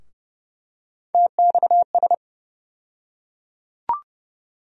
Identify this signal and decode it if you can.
Morse code, heard as a keyed tone, about 30 wpm, 700 Hz. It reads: TXS
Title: USA Call Sign Postfixes - 3 Letters 30wpm